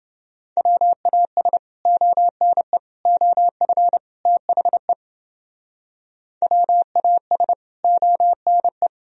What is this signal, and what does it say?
Morse code, heard as a keyed tone, about 32 wpm, 700 Hz. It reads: WAS ONE OF THE WAS ONE